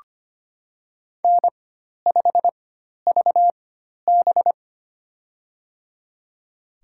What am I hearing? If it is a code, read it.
Morse code, heard as a keyed tone, about 25 wpm, spaced as for 12 wpm, 700 Hz. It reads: N5VB
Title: USA Call Signs - 4 Characters - 4x Character Spacing - 25wpm